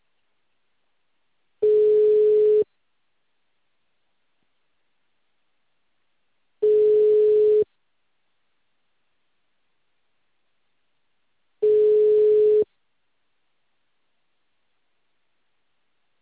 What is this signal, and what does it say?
Ringback tone (Europe, 425 Hz).